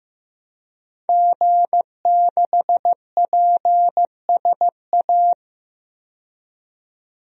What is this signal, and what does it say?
Morse code, heard as a keyed tone, about 15 wpm, 700 Hz. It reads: G6PSA
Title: DXCC-100 Call Signs - 5 Characters - 15wpm